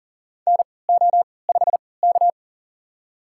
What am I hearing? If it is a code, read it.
Morse code, heard as a keyed tone, about 40 wpm, spaced as for 23 wpm, 700 Hz. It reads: NO5K
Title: USA Call Signs - 4 Characters - 3x Character Spacing - 40wpm